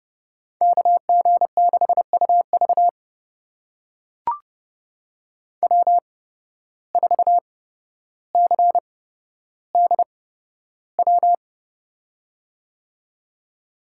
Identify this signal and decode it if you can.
Morse code, heard as a keyed tone, about 30 wpm, spaced as for 13 wpm, 700 Hz. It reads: KG6UV W4CDW
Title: USA Call Signs - 5 Characters - 8x Character Spacing - 30wpm